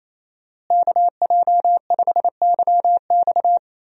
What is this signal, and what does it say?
Morse code, heard as a keyed tone, about 28 wpm, 700 Hz. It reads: KJ5YX
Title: USA Call Signs - 5 Characters - Repeated 2x - 28wpm